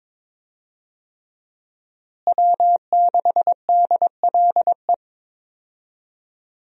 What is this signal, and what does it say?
Morse code, heard as a keyed tone, about 22 wpm, 700 Hz. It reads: W6DLE
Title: USA Call Signs - 5 Characters - Repeated 3x - 22wpm